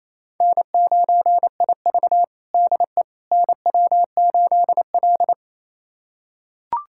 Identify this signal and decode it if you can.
Morse code, heard as a keyed tone, about 28 wpm, 700 Hz. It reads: N9IV DE NW8L
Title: USA Callsign DE Callsign - 4 Characters - 28wpm